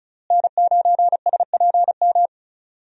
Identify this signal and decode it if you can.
Morse code, heard as a keyed tone, about 36 wpm, 700 Hz. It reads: N9SPM
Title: USA Call Signs - 5 Characters - Repeated 2x - 35wpm